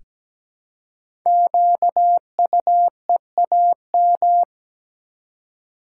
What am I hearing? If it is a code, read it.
Morse code, heard as a keyed tone, about 17 wpm, 700 Hz. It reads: QUEAM
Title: Code Groups - 5 Letters - 17wpm